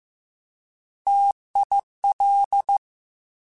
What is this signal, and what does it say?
Morse code, heard as a keyed tone, about 15 wpm, 800 Hz. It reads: TIL
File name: Alphabetical / Alphabetical mp3s / Ts @13